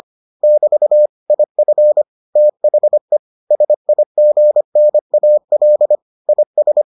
Morse code, heard as a keyed tone, about 25 wpm, 600 Hz. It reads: <BT> IF THE SIGNAL IS